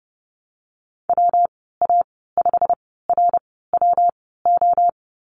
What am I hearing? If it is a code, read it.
Morse code, heard as a keyed tone, about 30 wpm, 700 Hz. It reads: W A 5 R W O
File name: Lesson 19g - Letter w - Call Signs 30wpm